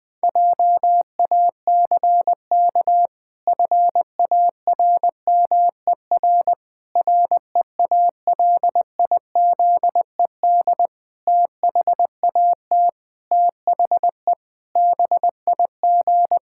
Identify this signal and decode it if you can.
Morse code, heard as a keyed tone, about 20 wpm, 700 Hz. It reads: JACK FARMER REALIZED THAT THE BIG
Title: Pangram Challenge v2 - 20wpm